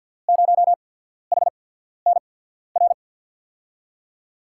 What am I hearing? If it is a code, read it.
Morse code, heard as a keyed tone, about 47 wpm, spaced as for 13 wpm, 700 Hz. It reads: 0HNR
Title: USA Call Signs - 6 Characters - 8x Character Spacing - 50wpm